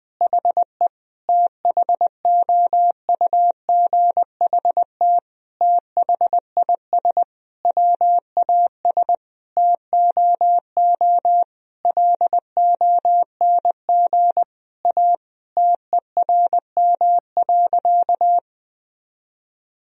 Morse code, heard as a keyed tone, about 20 wpm, 700 Hz. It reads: HE THOUGHT THIS WAS TOO LONG A TERM.
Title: Sentences from Top 1000 Words - Review Part 4 - 20wpm